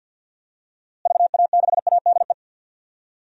Morse code, heard as a keyed tone, about 52 wpm, 700 Hz. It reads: VA6ADE
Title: DXCC-100 Call Signs - 6 Characters - Repeated 2x - 50wpm